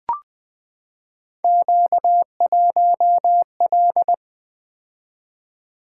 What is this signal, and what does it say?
Morse code, heard as a keyed tone, about 20 wpm, 700 Hz. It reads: Q1L
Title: Three Letter-Number Combinations - 20wpm